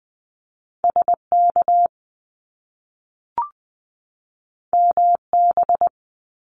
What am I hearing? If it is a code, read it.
Morse code, heard as a keyed tone, about 20 wpm, 700 Hz. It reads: SK MB
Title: Canadian Provinces and Territories 20wpm